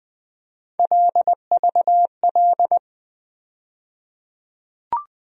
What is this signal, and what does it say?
Morse code, heard as a keyed tone, about 20 wpm, 700 Hz. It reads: LVL
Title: USA Call Sign Postfixes - 3 Letters 20wpm